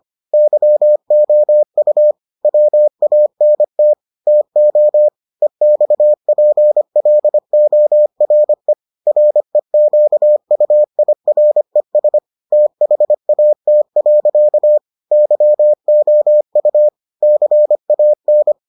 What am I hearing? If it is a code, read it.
Morse code, heard as a keyed tone, about 25 wpm, 600 Hz. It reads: YOU WANT TO EXPLORE REQUIRES THAT. YOU CAN